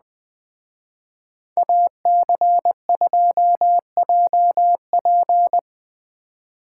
Morse code, heard as a keyed tone, about 20 wpm, 700 Hz. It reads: AC2JP